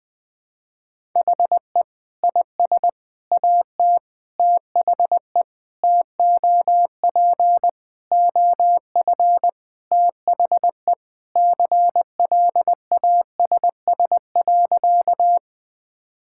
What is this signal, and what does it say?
Morse code, heard as a keyed tone, about 20 wpm, 700 Hz. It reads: HE IS AT THE TOP OF THE CLASS.
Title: Sentences with 3-Word Phrases - Spoken Beginning - 20wpm